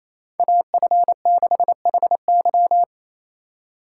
Morse code, heard as a keyed tone, about 28 wpm, 700 Hz. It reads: AF6HY
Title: USA Call Signs - 3x Character Spacing - 28wpm